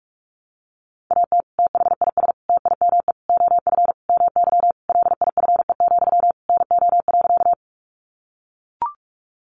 Morse code, heard as a keyed tone, about 44 wpm, 700 Hz. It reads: AT THIS TIME OF MY LIFE, NO.
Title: Sentences from Top 300 Words 45wpm